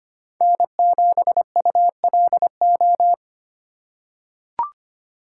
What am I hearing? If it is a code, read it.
Morse code, heard as a keyed tone, about 25 wpm, 700 Hz. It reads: N7ULO
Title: USA Call Signs - 5 Characters - 4x Character Spacing - 25wpm